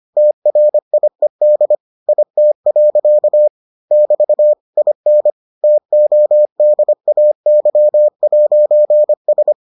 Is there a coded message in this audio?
Yes: Morse code from 0.2 to 9.5 s, heard as a keyed tone, about 25 wpm, 600 Hz. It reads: TRIED IT. <BT> IN TODAY'S